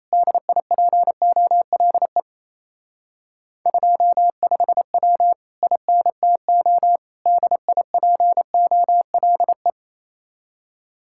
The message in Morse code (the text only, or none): DIPOLE 25W INTO DIPOLE